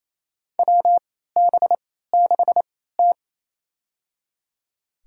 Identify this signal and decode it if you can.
Morse code, heard as a keyed tone, about 28 wpm, spaced as for 16 wpm, 700 Hz. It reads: WB6T
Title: USA Call Signs - 3x Character Spacing - 28wpm